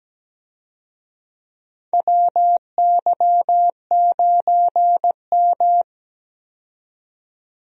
Morse code, heard as a keyed tone, about 17 wpm, 700 Hz. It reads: WY9M
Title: USA Call Signs - 4 Characters - 17wpm